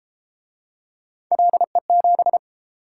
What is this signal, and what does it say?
Morse code, heard as a keyed tone, about 29 wpm, 700 Hz. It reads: LE7